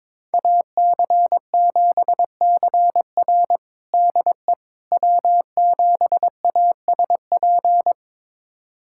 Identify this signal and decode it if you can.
Morse code, heard as a keyed tone, about 22 wpm, 700 Hz. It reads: AC7CR DE W7ASP
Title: USA Callsign DE Callsign - 5 Characters - 22wpm